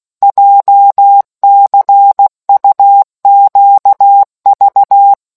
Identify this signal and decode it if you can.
Morse code, heard as a keyed tone, about 16 wpm, 800 Hz. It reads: JCUQV